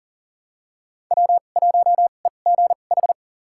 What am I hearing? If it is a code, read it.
Morse code, heard as a keyed tone, about 40 wpm, spaced as for 29 wpm, 700 Hz. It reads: W1EGH
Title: USA Call Signs - 2x Character Spacing - 40wpm